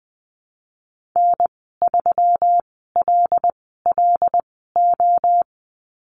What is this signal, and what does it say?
Morse code, heard as a keyed tone, about 20 wpm, spaced as for 14 wpm, 700 Hz. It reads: N3LLO
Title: 219 Lesson 35g - Number 0 - Call Signs 20wpm